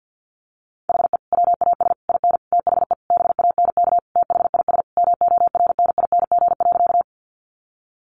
Morse code, heard as a keyed tone, about 50 wpm, 700 Hz. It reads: HE WAS IN THE BANK THIS MORNING.
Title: Sentences from Top 900 Words 50wpm